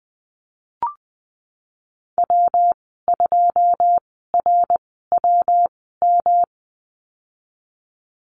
Morse code, heard as a keyed tone, about 20 wpm, spaced as for 15 wpm, 700 Hz. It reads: W2RWM
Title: 128 Lesson 21g - Number 6 - Call Signs 20wpm